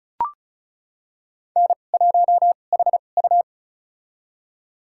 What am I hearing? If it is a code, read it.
Morse code, heard as a keyed tone, about 36 wpm, spaced as for 25 wpm, 700 Hz. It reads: N1HU
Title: USA Call Signs - 2x Character Spacing - 35wpm